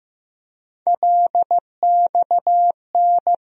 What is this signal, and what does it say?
Morse code, heard as a keyed tone, about 15 wpm, 700 Hz. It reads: LXN
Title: USA Call Sign Postfixes - 3 Letters 15wpm